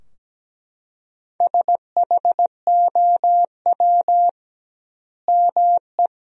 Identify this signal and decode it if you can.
Morse code, heard as a keyed tone, about 17 wpm, 700 Hz. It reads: SHOW ME
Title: 2x Word-Spacing - Sentences from Top 200 Words - 17wpm